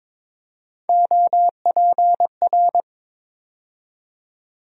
Morse code, heard as a keyed tone, about 22 wpm, 700 Hz. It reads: OPR
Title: USA Call Sign Postfixes - 3 Letters 22wpm